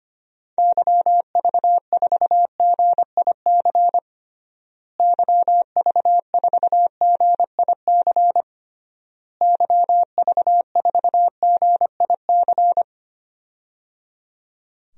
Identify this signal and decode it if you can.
Morse code, heard as a keyed tone, about 25 wpm, 700 Hz. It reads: YV4GIC YV4GIC YV4GIC
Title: DXCC-100 Call Signs - 6 Characters - Repeated 3x - 25wpm